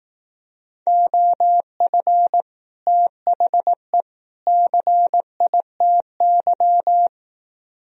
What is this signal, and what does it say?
Morse code, heard as a keyed tone, about 18 wpm, 700 Hz. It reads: OF THE CITY
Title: Speed-Racing - Sets of 3 Words 15wpm